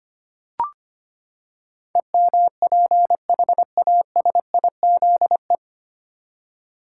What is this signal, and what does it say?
Morse code, heard as a keyed tone, about 25 wpm, 700 Hz. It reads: EMPHASIZE
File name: Nine Letter Words - 25wpm